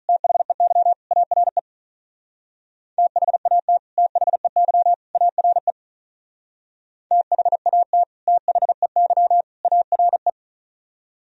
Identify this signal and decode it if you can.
Morse code, heard as a keyed tone, about 43 wpm, 700 Hz. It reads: THEY ARE THAT THEY ARE THAT THEY ARE
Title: Speed-Racing - Sets of 3 Words 35wpm